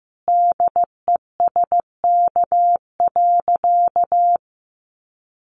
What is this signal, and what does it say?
Morse code, heard as a keyed tone, about 15 wpm, 700 Hz. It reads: DESK.
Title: Easy Phrases 15wpm